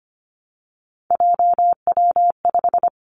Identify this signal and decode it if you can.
Morse code, heard as a keyed tone, about 25 wpm, 700 Hz. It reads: JW5
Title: Three Challenging Letter-Number Combinations 25wpm